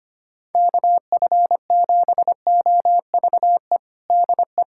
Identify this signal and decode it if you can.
Morse code, heard as a keyed tone, about 25 wpm, 700 Hz. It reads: KF7OVE DE
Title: USA Callsign DE Callsign - 6 Characters - 25wpm